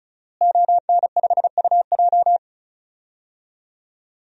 Morse code, heard as a keyed tone, about 36 wpm, 700 Hz. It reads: ON5UJ
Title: DXCC-100 Call Signs - 5 Characters - Repeated 2x - 35wpm